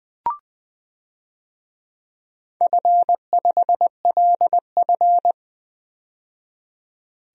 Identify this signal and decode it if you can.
Morse code, heard as a keyed tone, about 20 wpm, 700 Hz. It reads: F5LF
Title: DXCC-100 Call Signs - 4 Characters - 20wpm